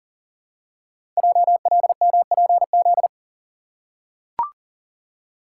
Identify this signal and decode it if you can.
Morse code, heard as a keyed tone, about 42 wpm, 700 Hz. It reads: JLMPZ
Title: Code Groups - 5 Letters - 40wpm